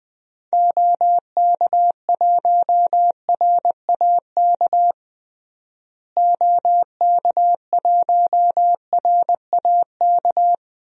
Morse code, heard as a keyed tone, about 20 wpm, 700 Hz. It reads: OK1RAK OK1RAK
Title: DXCC-100 Call Signs - 6 Characters - Repeated 3x - 20wpm